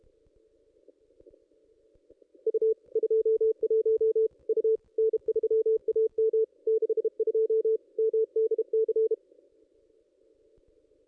Morse code, heard as a keyed tone, about 37 wpm, 440 Hz. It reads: U 21 U N3AM 62 MDC
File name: contests / ss-cw-k5tr-op-n5rz -1900